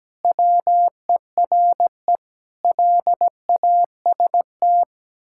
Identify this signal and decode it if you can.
Morse code, heard as a keyed tone, about 17 wpm, 700 Hz. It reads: WERE LAST
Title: Pangram Challenge v2 - 17wpm